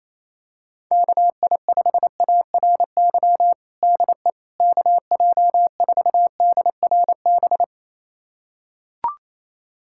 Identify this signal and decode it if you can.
Morse code, heard as a keyed tone, about 28 wpm, 700 Hz. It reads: KI5ARY DE KJ4DRB
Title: USA Callsign DE Callsign 28wpm